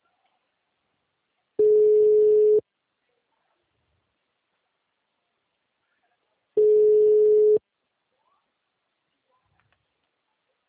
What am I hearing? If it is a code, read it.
Ringback tone (Europe, 425 Hz).